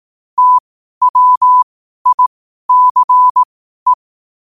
Morse code, heard as a keyed tone, about 18 wpm, spaced as for 12 wpm, 1000 Hz. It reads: TWICE